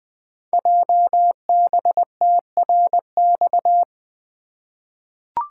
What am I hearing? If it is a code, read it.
Morse code, heard as a keyed tone, about 20 wpm, 700 Hz. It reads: JBTRX
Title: Code Groups - 5 Letters - 20wpm